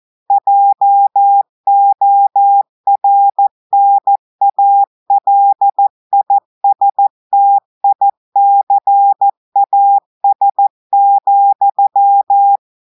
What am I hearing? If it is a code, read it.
Morse code, heard as a keyed tone, about 14 wpm, 800 Hz. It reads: JORNALISTICAS,
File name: audios / portugais / portugais texte en clair 2 vitesse 14 mots minutes